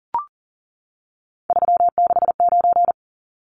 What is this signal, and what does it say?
Morse code, heard as a keyed tone, about 40 wpm, 700 Hz. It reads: 369
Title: Numbers - 3-Digit 40wpm